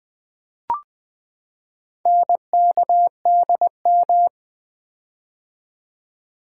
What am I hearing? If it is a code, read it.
Morse code, heard as a keyed tone, about 20 wpm, 700 Hz. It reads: NKDM
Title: Four Letter-Number Combinations - 20wpm